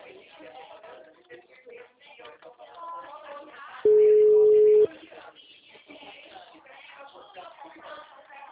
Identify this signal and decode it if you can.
Ringback tone (Europe, 425 Hz), with speech.